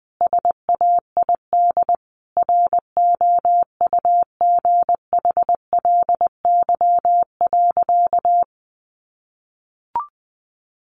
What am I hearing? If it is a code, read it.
Morse code, heard as a keyed tone, about 20 wpm, 700 Hz. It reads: SAID ROUGHLY.
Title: Sentences from Top 1500 Words 20wpm